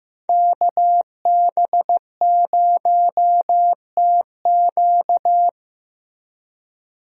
Morse code, heard as a keyed tone, about 15 wpm, 700 Hz. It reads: KB0TQ
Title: USA Call Signs - 5 Characters - Repeated 3x - 15wpm